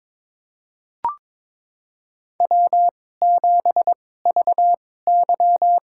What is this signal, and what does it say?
Morse code, heard as a keyed tone, about 23 wpm, spaced as for 16 wpm, 700 Hz. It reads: W7VY
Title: USA Call Signs - 4 Characters - 2x Character Spacing - 22wpm